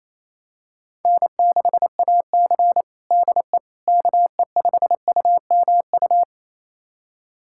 Morse code, heard as a keyed tone, about 28 wpm, 700 Hz. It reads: N6AC DE KE5UMU
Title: USA Callsign DE Callsign 28wpm